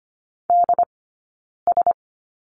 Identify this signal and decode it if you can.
Morse code, heard as a keyed tone, about 25 wpm, spaced as for 9 wpm, 690 Hz. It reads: DS
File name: Lesson 15d - Number 5 - Two Characters 25wpm